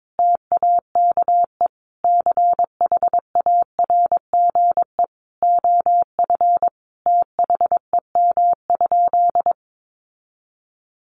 Morse code, heard as a keyed tone, about 22 wpm, 700 Hz. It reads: TAKE CHARGE OF THEM?
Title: Sentences from Top 900 Words 22wpm